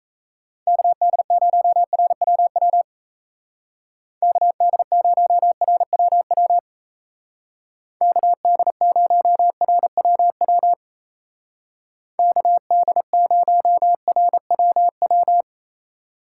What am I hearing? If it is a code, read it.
Morse code, heard as a keyed tone, about 35 wpm, 700 Hz. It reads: KD0RWW KD0RWW KD0RWW KD0RWW